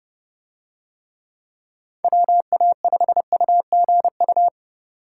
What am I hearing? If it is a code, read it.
Morse code, heard as a keyed tone, about 30 wpm, 700 Hz. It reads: WA5UGU